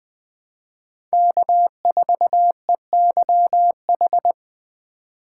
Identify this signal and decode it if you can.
Morse code, heard as a keyed tone, about 20 wpm, 700 Hz. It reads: K4EYH
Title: USA Call Signs - 5 Characters - 2x Character Spacing - 20wpm